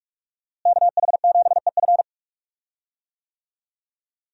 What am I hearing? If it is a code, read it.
Morse code, heard as a keyed tone, about 47 wpm, 700 Hz. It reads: KH7EF
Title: USA Call Signs - 5 Characters - Repeated 2x - 45wpm